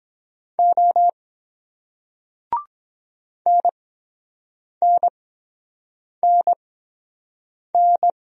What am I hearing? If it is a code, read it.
Morse code, heard as a keyed tone, about 23 wpm, spaced as for 7 wpm, 700 Hz. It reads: O NNNN